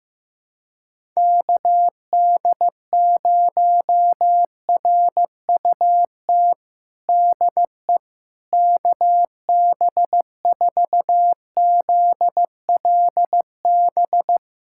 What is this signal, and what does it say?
Morse code, heard as a keyed tone, about 15 wpm, 700 Hz. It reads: KD0RUT DE KB4ZLB
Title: USA Callsign DE Callsign - 6 Characters - 15wpm